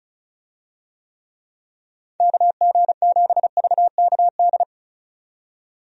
Morse code, heard as a keyed tone, about 36 wpm, 700 Hz. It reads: KG7VKD
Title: USA Call Signs - 6 Characters - 35wpm